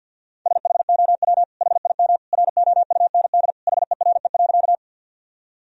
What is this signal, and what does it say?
Morse code, heard as a keyed tone, about 50 wpm, 700 Hz. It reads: SHOW HIM ROUND HERE.
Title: Sentences from Top 1000 Words - Review Part 1 - 50wpm